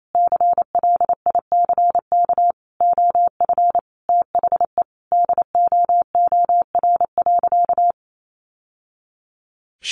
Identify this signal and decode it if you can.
Morse code, heard as a keyed tone, about 28 wpm, 700 Hz. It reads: CLICK OF THE DOOR.
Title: Sentences from Top 1100 Words 28wpm